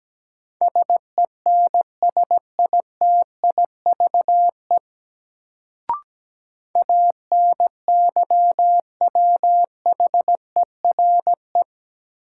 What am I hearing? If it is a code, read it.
Morse code, heard as a keyed tone, about 17 wpm, 700 Hz. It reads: SENSITIVE ANYWHERE
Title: Sentences from Top 1400 Words - New Word Drill - 17wpm